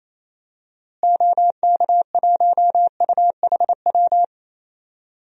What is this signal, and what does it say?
Morse code, heard as a keyed tone, about 28 wpm, 700 Hz. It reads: OK1UHW
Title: DXCC-100 Call Signs - 6 Characters - Repeated 3x - 28wpm